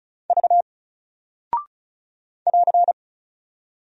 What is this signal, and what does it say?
Morse code, heard as a keyed tone, about 36 wpm, 700 Hz. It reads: V <AR>
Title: Single Character (Letter:Number:Punctuation:Prosign) Alt 35wpm